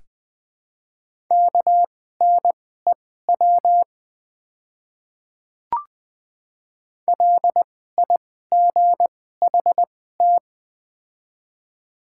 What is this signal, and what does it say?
Morse code, heard as a keyed tone, about 21 wpm, spaced as for 15 wpm, 700 Hz. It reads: KNEW LIGHT
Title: 2x Character Spacing - Top 500 Words - 20wpm